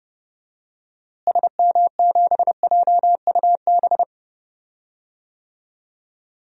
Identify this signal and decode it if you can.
Morse code, heard as a keyed tone, about 30 wpm, 700 Hz. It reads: SM7JUB
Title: DXCC-100 Call Signs - 6 Characters - Repeated 2x - 30wpm